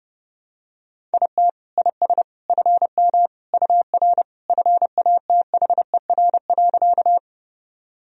Morse code, heard as a keyed tone, about 30 wpm, 700 Hz. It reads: IT IS FM UR FATHER.
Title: Sentences from Top 300 Words and Abbreviations 30wpm